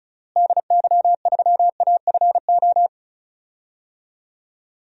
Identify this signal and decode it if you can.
Morse code, heard as a keyed tone, about 34 wpm, 700 Hz. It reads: DY3AFO
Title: DXCC-100 Call Signs - 6 Characters - Repeated 2x - 35wpm